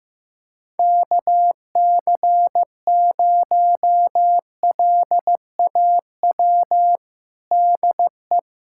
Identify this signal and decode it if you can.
Morse code, heard as a keyed tone, about 15 wpm, 700 Hz. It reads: KC0LAW DE
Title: USA Callsign DE Callsign - 6 Characters - 15wpm